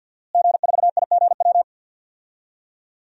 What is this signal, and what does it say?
Morse code, heard as a keyed tone, about 51 wpm, 700 Hz. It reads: G4IGW
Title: DXCC-100 Call Signs - 5 Characters - Repeated 2x - 50wpm